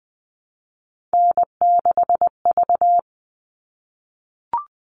Morse code, heard as a keyed tone, about 20 wpm, 700 Hz. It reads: N6V
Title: All Cut Numbers - 3-Digit 20wpm